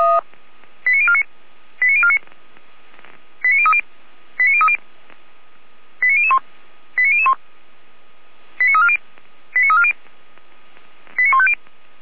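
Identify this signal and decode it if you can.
Five-tone selective call (ZVEI), heard as quick runs of five tones, tones of about 80 ms. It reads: 89049 89049 89030 89030 89001 89001 89340 89340 89150
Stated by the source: ZVEI five-tone selective call 89310, received on an SDR.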